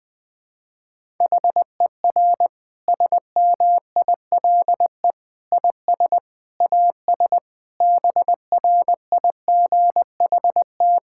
Morse code, heard as a keyed tone, about 20 wpm, 700 Hz. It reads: HER SMILE IS AS BRIGHT
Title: Difficult Sentences with Hints - 20wpm